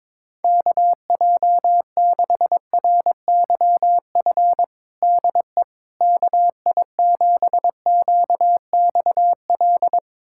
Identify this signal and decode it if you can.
Morse code, heard as a keyed tone, about 22 wpm, 700 Hz. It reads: KJ6RYF DE KI7QXL
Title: USA Callsign DE Callsign - 6 Characters - 22wpm